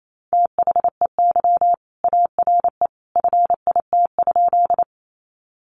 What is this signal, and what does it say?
Morse code, heard as a keyed tone, about 28 wpm, 700 Hz. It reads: THEY ARE FIT?
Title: Sentences from Top 800 Words 28wpm